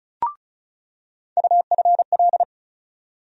Morse code, heard as a keyed tone, about 36 wpm, 700 Hz. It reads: UFL
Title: USA Call Sign Postfixes - 3 Letters 35wpm